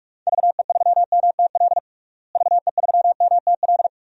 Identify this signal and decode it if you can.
Morse code, heard as a keyed tone, about 43 wpm, 700 Hz. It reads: VE3MTL VE3MTL
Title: DXCC-100 Call Signs - 6 Characters - Repeated 2x - 45wpm